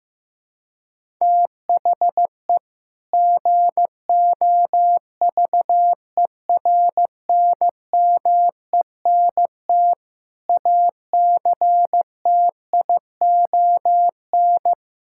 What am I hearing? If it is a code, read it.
Morse code, heard as a keyed tone, about 15 wpm, 700 Hz. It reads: THE GOVERNMENT ACTION
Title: Sentences from Top 1000 Words - Review Part 5 - 15wpm